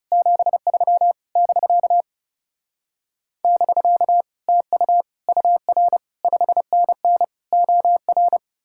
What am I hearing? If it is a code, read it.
Morse code, heard as a keyed tone, about 39 wpm, 700 Hz. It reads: 73 <BK> <BK> TU UR 5NN OR